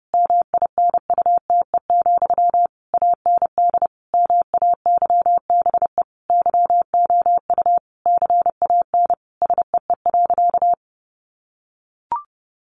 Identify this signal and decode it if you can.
Morse code, heard as a keyed tone, about 30 wpm, 700 Hz. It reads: MINUTE, AND MAYBE YOU CAN SEE.
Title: Sentences from Top 1000 Words 30wpm